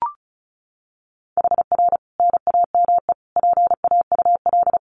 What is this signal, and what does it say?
Morse code, heard as a keyed tone, about 34 wpm, 700 Hz. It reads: HR NAME PAUL
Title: Common QSO Phrases 35wpm